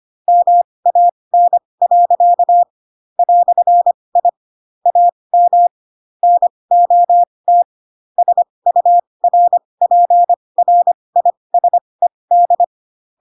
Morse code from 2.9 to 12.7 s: "I AM NOT SURPRISED